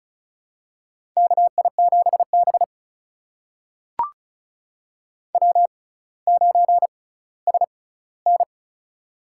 Morse code, heard as a keyed tone, about 36 wpm, spaced as for 18 wpm, 700 Hz. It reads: KI7B W9SN
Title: USA Call Signs - 4 Characters - 6x Character Spacing - 35wpm